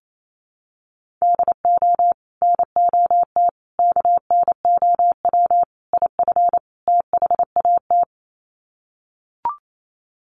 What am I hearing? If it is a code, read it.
Morse code, heard as a keyed tone, about 28 wpm, 700 Hz. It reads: DO NOT KNOW IF THAT
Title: Sets of 5 Words Encore2 28wpm